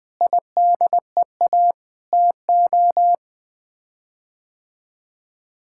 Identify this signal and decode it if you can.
Morse code, heard as a keyed tone, about 21 wpm, 700 Hz. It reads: IDEA TO
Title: Sets of 9 Words - 20wpm